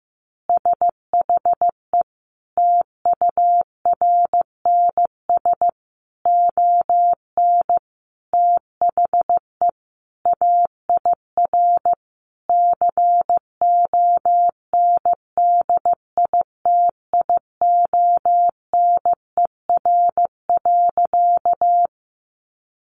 Morse code, heard as a keyed tone, about 15 wpm, 700 Hz. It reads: SHE TURNS ON THE AIR CONDITIONER.
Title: Easy Phrases 15wpm